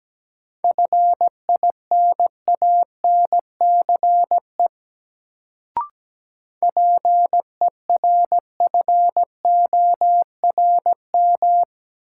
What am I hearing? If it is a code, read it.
Morse code, heard as a keyed tone, about 17 wpm, 700 Hz. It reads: FINANCE PERFORM
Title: Sentences from Top 1500 Words - New Word Drill Alt - 17wpm